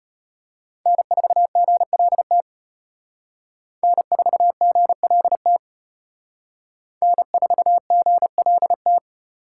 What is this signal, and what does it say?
Morse code, heard as a keyed tone, about 32 wpm, 700 Hz. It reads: N4GLT N4GLT N4GLT